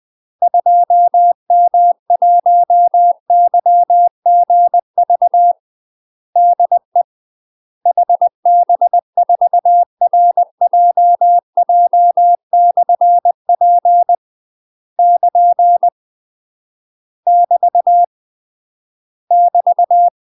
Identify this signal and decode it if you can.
Morse code, heard as a keyed tone, about 20 wpm, 700 Hz. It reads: 2M1YGV DE HB4RJJ/P <KN> <BT> <BT>
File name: UK MASTER QSOs TRAINING / 20WPM / UK MASTER QSOs 0071